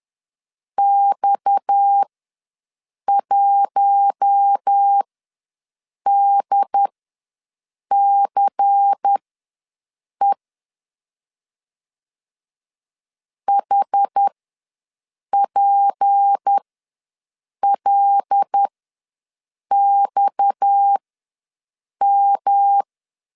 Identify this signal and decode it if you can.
Morse code, heard as a keyed tone, about 10 wpm, spaced as for 6 wpm, 800 Hz. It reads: X1DCE HPLXM
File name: LEZIONI / 6 PKX0 / 6 PKX ws 10wpm 3706